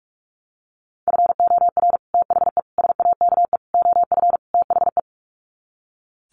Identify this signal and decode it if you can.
Morse code, heard as a keyed tone, about 44 wpm, 700 Hz. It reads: FOR THE SAKE OF THE
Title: Sets of 5 Words Encore 45wpm